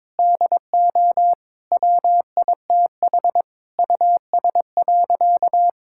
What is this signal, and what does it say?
Morse code, heard as a keyed tone, about 22 wpm, 700 Hz. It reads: DO WITH US.
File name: practice / 22wpm / Sentences with 8-Word Phrases - 22wpm